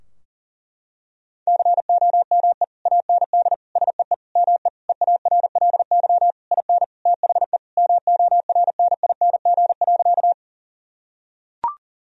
Morse code, heard as a keyed tone, about 40 wpm, 700 Hz. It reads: COME AND SEE ME EARLY IN THE MORNING.
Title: Sentences from Top 1000 Words - Review Part 2 - 40wpm